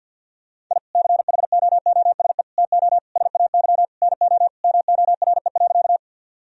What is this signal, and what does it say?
Morse code, heard as a keyed tone, about 50 wpm, 700 Hz. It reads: I CHOOSE TO SAY NO MORE.
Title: Sentences from Top 1000 Words - Review Part 4 - 50wpm